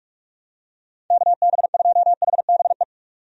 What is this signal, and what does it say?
Morse code, heard as a keyed tone, about 46 wpm, 700 Hz. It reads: KB2HBE
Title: USA Call Signs - 6 Characters - 45wpm